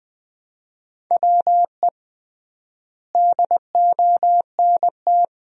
Morse code, heard as a keyed tone, about 20 wpm, 700 Hz. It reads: WE DONT
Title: 3x Word-Spacing - Sentences from Top 400 Words - 20wpm